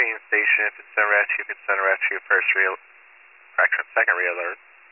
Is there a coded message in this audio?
Yes: Emergency Alert System (SAME) tones from 0.7 to 4.2 s, heard as harsh buzzing bursts, with speech.